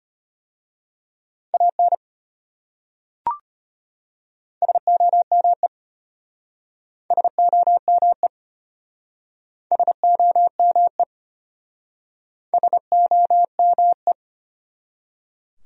Morse code, heard as a keyed tone, about 33 wpm, 700 Hz. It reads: AN SOME SOME SOME SOME